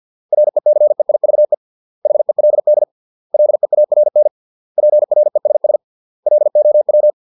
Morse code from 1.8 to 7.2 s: HELD LEARN PRESS LOW